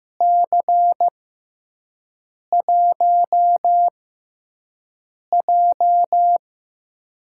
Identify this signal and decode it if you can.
Morse code, heard as a keyed tone, about 15 wpm, spaced as for 5 wpm, 700 Hz. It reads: C1J
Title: USA Call Signs - 4 Characters - 6x Character Spacing - 15wpm